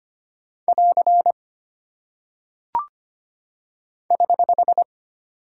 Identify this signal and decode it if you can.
Morse code, heard as a keyed tone, about 25 wpm, 700 Hz. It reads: <AR> <HH>
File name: Single Punctuation-Prosign - 25wpm